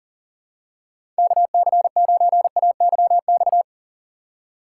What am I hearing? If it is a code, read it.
Morse code, heard as a keyed tone, about 40 wpm, 700 Hz. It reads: KC9AYX
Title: USA Call Signs - 6 Characters - 3x Character Spacing - 40wpm